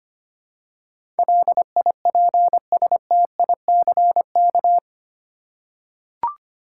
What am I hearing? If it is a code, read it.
Morse code, heard as a keyed tone, about 25 wpm, 700 Hz. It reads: LIPSTICK
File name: practice / 25wpm / Easy Words with Hints - 25wpm